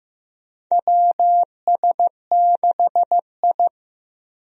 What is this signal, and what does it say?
Morse code, heard as a keyed tone, about 15 wpm, 700 Hz. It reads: WS6I
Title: USA Call Signs - 4 Characters - 15wpm